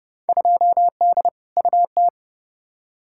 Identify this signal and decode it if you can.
Morse code, heard as a keyed tone, about 30 wpm, 700 Hz. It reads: 2D UT
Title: Field Day Exchanges Encore - 30wpm